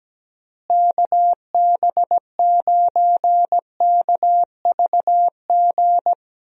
Morse code, heard as a keyed tone, about 17 wpm, 700 Hz. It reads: KB9KVG
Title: USA Call Signs - 6 Characters - Repeated 3x - 17wpm